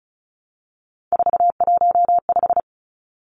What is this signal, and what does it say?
Morse code, heard as a keyed tone, about 34 wpm, 700 Hz. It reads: 415
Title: Numbers - 3-Digit 35wpm